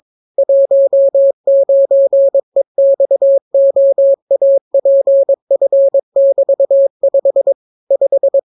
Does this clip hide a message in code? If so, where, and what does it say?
Yes: Morse code from 0.4 to 8.4 s, heard as a keyed tone, about 22 wpm, 550 Hz. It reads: 19EXOAPF<BT>5 5